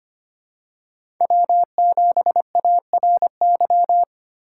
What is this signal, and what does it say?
Morse code, heard as a keyed tone, about 25 wpm, 700 Hz. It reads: W7ARY
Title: USA Call Signs - 5 Characters - 6x Character Spacing - 25wpm